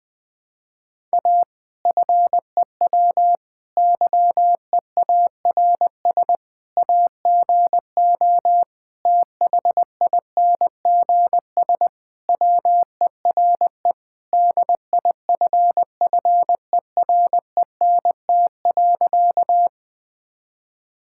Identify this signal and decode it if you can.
Morse code, heard as a keyed tone, about 20 wpm, 700 Hz. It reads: A FEW YEARS AGO THINGS WERE DIFFERENT.
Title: Sentences with 4-Word Phrases - 20wpm